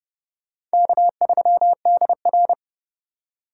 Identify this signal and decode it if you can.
Morse code, heard as a keyed tone, about 28 wpm, 700 Hz. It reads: K3DR